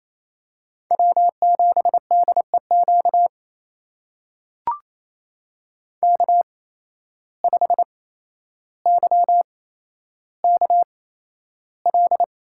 Morse code, heard as a keyed tone, about 28 wpm, spaced as for 12 wpm, 700 Hz. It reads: W7DEQ K5YKL
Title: USA Call Signs - 5 Characters - 8x Character Spacing - 28wpm